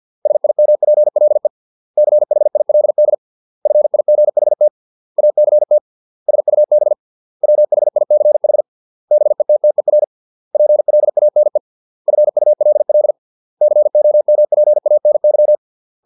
Morse code, heard as a keyed tone, about 50 wpm, 600 Hz. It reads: SIMPLE CHILD FIGHT ACT SUB WHICH BETTER PLANE FULL COMPANY
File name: 632 words / 632 Words 50WPM